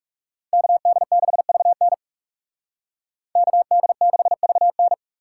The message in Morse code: KD6VN KD6VN

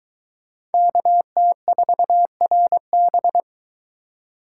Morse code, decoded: KT4RB

